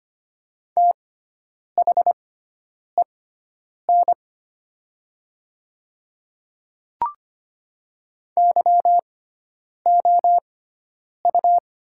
Morse code, heard as a keyed tone, about 25 wpm, spaced as for 9 wpm, 700 Hz. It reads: THEN YOU
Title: 6x Character Spacing - Top 100 Words - 25wpm